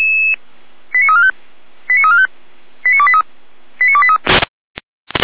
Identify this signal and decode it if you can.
Five-tone selective call (ZVEI), heard as quick runs of five tones, tones of about 70 ms. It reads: 89345 89345 89283 89283
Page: feuerwehr-orlamuende